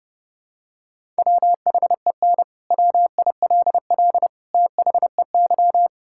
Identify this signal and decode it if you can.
Morse code, heard as a keyed tone, about 30 wpm, 700 Hz. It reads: WHEN WILL THEY
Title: Sentences from Top 1000 Words - Review Part 6 - 30wpm